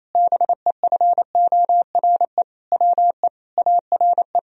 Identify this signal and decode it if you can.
Morse code, heard as a keyed tone, about 28 wpm, 700 Hz. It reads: BEFORE WE ARE
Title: Sentences from Top 1000 Words - Review Part 1 - 28wpm